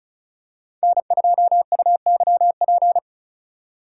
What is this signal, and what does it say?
Morse code, heard as a keyed tone, about 36 wpm, 700 Hz. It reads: N2UYP
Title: USA Call Signs - 8x Character Spacing - 35wpm